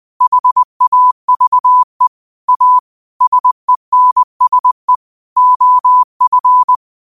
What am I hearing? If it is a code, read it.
Morse code, heard as a keyed tone, about 20 wpm, 1000 Hz. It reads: HAVE A SENSE OF
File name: qotdfiles / 20wpm